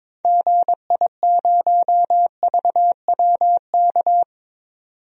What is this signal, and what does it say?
Morse code, heard as a keyed tone, about 22 wpm, 700 Hz. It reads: GI0VWK
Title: DXCC-100 Call Signs - 6 Characters - 22wpm